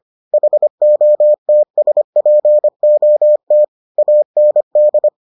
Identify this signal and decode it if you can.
Morse code, heard as a keyed tone, about 25 wpm, 600 Hz. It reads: HOTSPOT AND